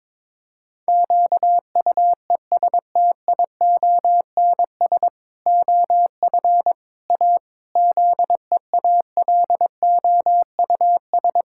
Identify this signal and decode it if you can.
Morse code, heard as a keyed tone, about 22 wpm, 700 Hz. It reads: QUESTIONS OF A ZEALOUS
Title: Pangram Challenge v2 - 22wpm